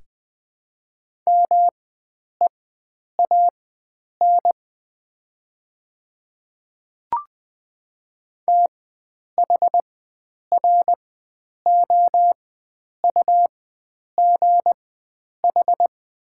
Morse code, heard as a keyed tone, about 21 wpm, spaced as for 9 wpm, 700 Hz. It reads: MEAN THROUGH
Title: 4x Character Spacing - Top 300 Words - 20wpm